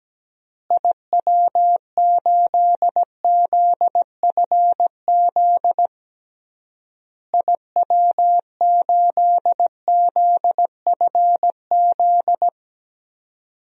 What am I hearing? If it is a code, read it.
Morse code, heard as a keyed tone, about 17 wpm, 700 Hz. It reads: IW8ZFZ IW8ZFZ
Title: DXCC-100 Call Signs - 6 Characters - Repeated 3x - 17wpm